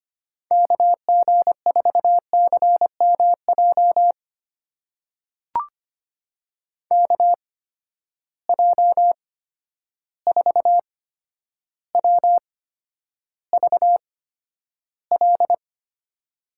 Morse code, heard as a keyed tone, about 25 wpm, spaced as for 11 wpm, 700 Hz. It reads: KG4CMJ KJ4WVL
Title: USA Call Signs - 6 Characters - 8x Character Spacing - 25wpm